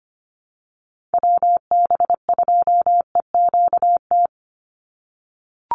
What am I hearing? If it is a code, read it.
Morse code, heard as a keyed tone, about 25 wpm, 700 Hz. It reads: WB2EQT
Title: USA Call Signs 25wpm